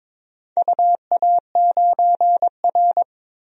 Morse code, heard as a keyed tone, about 22 wpm, 700 Hz. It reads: UA9R
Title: DXCC-100 Call Signs - 4 Characters - Repeated 3x - 22wpm